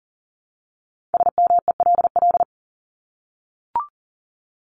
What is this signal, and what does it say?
Morse code, heard as a keyed tone, about 40 wpm, 700 Hz. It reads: SMELL
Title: Top 1501-1750 Words 40wpm